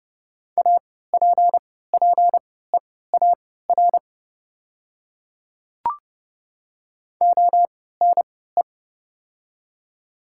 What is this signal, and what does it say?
Morse code, heard as a keyed tone, about 30 wpm, spaced as for 17 wpm, 700 Hz. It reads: APPEAR ONE
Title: 3x Character Spacing - Top 500 Words - 30wpm